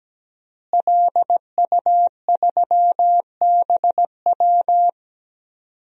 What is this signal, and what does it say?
Morse code, heard as a keyed tone, about 17 wpm, 700 Hz. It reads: LU3BW
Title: DXCC-100 Call Signs - 5 Characters - Repeated 3x - 17wpm